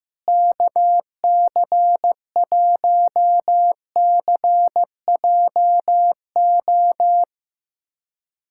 Morse code, heard as a keyed tone, about 15 wpm, 700 Hz. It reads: KC1CJO
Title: USA Callsign DE Callsign 15wpm